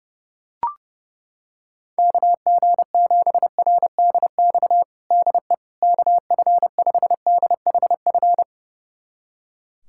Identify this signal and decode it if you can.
Morse code, heard as a keyed tone, about 30 wpm, 700 Hz. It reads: KG7RDX DE KF5DHF
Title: USA Callsign DE Callsign - 6 Characters - 30wpm